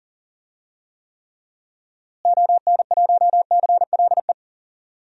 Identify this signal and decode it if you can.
Morse code, heard as a keyed tone, about 40 wpm, 700 Hz. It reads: ON1CLE
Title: DXCC-100 Call Signs - 6 Characters - Repeated 2x - 40wpm